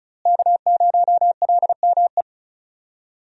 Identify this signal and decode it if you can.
Morse code, heard as a keyed tone, about 36 wpm, 700 Hz. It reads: K0LME
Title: USA Call Signs - 5 Characters - Repeated 2x - 35wpm